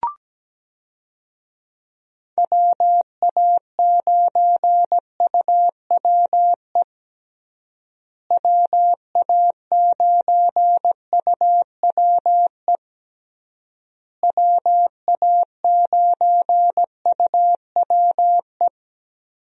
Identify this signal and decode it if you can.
Morse code, heard as a keyed tone, about 17 wpm, 700 Hz. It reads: WA9UWE WA9UWE WA9UWE
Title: USA Call Signs - 6 Characters - Repeated 3x - 17wpm